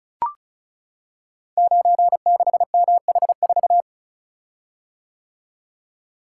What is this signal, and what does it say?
Morse code, heard as a keyed tone, about 36 wpm, 700 Hz. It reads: 96MH4
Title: Code Groups - 5 Letter:Numbers - 35wpm